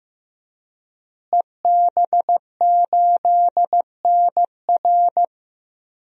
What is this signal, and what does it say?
Morse code, heard as a keyed tone, about 15 wpm, 700 Hz. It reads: EB8NR
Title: DXCC-100 Call Signs - 5 Characters - Repeated 2x - 15wpm